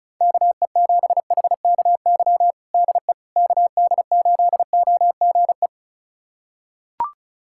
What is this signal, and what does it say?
Morse code, heard as a keyed tone, about 34 wpm, 700 Hz. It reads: KE7HKY DE KD8OGE
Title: USA Callsign DE Callsign - 6 Characters - 35wpm